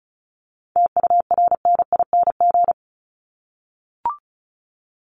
Morse code, heard as a keyed tone, about 34 wpm, 700 Hz. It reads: TURNING
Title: ING Suffix - Top 100 Words with Base Words 35wpm